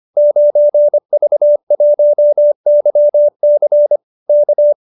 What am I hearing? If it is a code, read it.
Morse code, heard as a keyed tone, about 25 wpm, 590 Hz. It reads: 9V1YC K
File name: qso207b 25